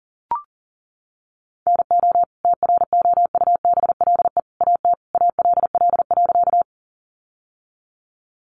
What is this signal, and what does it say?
Morse code, heard as a keyed tone, about 40 wpm, 700 Hz. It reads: NO TROUBLE AT ALL.
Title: Sentences from Top 1200 Words 40wpm